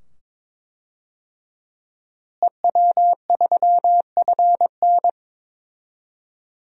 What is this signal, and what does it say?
Morse code, heard as a keyed tone, about 22 wpm, 700 Hz. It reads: EW3FN
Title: DXCC-100 Call Signs - 5 Characters - Repeated 2x - 22wpm